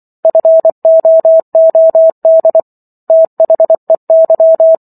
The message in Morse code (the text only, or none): FOOD THEY